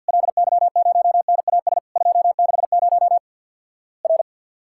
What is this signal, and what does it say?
Morse code, heard as a keyed tone, about 50 wpm, 700 Hz. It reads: LY0NAS 260 R
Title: Worked All Europe DX Contest - QTC Exchange with Repeats - 50wpm